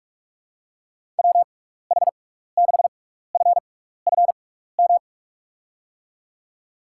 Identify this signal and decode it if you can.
Morse code, heard as a keyed tone, about 46 wpm, spaced as for 16 wpm, 700 Hz. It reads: WH6FFM
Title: USA Call Signs - 6x Character Spacing - 45wpm